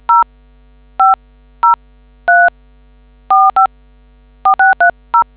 DTMF, heard as telephone touch tones, tones of about 150 ms. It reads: 0503454630